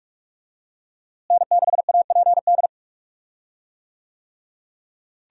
Morse code, heard as a keyed tone, about 45 wpm, 700 Hz. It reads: N6APD
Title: USA Call Signs - 5 Characters - Repeated 2x - 45wpm